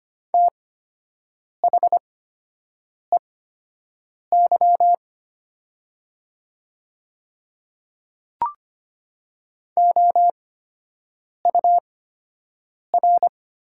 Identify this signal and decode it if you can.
Morse code, heard as a keyed tone, about 25 wpm, spaced as for 7 wpm, 700 Hz. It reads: THEY OUR
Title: 8x Character Spacing - Top 200 Words - 25wpm